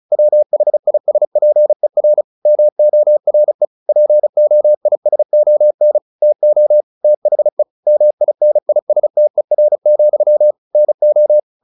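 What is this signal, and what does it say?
Morse code, heard as a keyed tone, about 37 wpm, 600 Hz. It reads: WHISPER MORE POISON TO THE MINISTER, NO
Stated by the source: Morse practice at 35 wpm from CW med Gnister, sending Ken Follett's novel Never in English.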